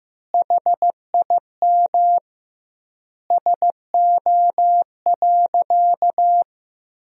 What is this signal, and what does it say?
Morse code, heard as a keyed tone, about 15 wpm, 700 Hz. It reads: HIM SO.
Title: 2x Word-Spacing - Sentences from Top 200 Words - 15wpm